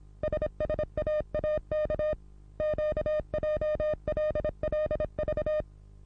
Morse code, heard as a keyed tone, about 26 wpm, 610 Hz. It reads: SSAAK QJLLV
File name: repos / Morsekurs / DJ4UF Morsen27